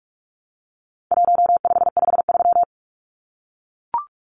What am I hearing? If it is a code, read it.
Morse code, heard as a keyed tone, about 44 wpm, 700 Hz. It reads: 1553
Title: Numbers - 4-Digit 45wpm